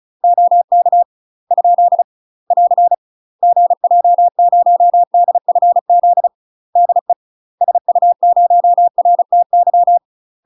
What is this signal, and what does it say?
Morse code, heard as a keyed tone, about 36 wpm, 700 Hz. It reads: OK ? <AR> GJ0DFZ DE SU0RTY
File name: UK MASTER QSOs TRAINING / 35WPM / UK MASTER QSOs 0040